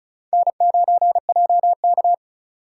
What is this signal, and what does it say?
Morse code, heard as a keyed tone, about 36 wpm, 700 Hz. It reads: N9JK
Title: USA Call Signs - 4 Characters - 3x Character Spacing - 35wpm